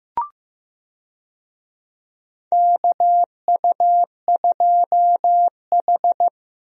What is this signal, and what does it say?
Morse code, heard as a keyed tone, about 15 wpm, 700 Hz. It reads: KU2H
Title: USA Call Signs - 4 Characters - Repeated 3x - 15wpm